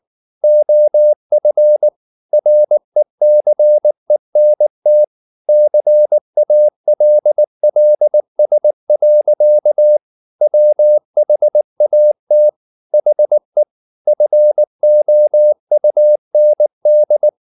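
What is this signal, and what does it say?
Morse code, heard as a keyed tone, about 19 wpm, 600 Hz. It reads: OF RECENT CALLS. WHAT HE FOUND
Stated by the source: Morse code at 19 wpm from CW med Gnister, reading Ken Follett's novel Never in English.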